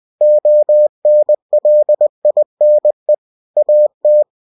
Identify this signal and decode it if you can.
Morse code, heard as a keyed tone, about 20 wpm, 600 Hz. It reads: ONLINE AT
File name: cw-books / winnie-the-pooh / winnie-20wpm-page-0113